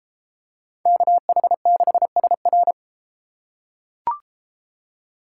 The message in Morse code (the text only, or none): KH6SR